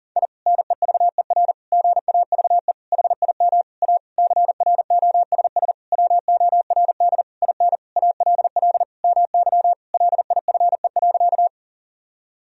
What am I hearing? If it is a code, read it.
Morse code, heard as a keyed tone, about 37 wpm, 700 Hz. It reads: I NEVER GAVE HIM A CROSS WORD IN ALL MY LIFE.
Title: Sentences from Top 1000 Words - Review Part 1 - 40wpm